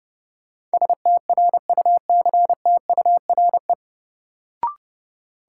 Morse code, heard as a keyed tone, about 30 wpm, 700 Hz. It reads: STRUCTURE
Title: Sentences from Top 1000 Words - Review Part 6 - New Word Drill - 30wpm